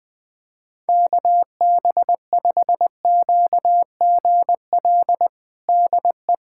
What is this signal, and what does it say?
Morse code, heard as a keyed tone, about 21 wpm, 700 Hz. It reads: KB5QGL DE
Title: USA Callsign DE Callsign - 6 Characters - 20wpm